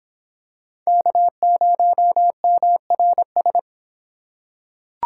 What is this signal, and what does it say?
Morse code, heard as a keyed tone, about 26 wpm, 700 Hz. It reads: K0MRS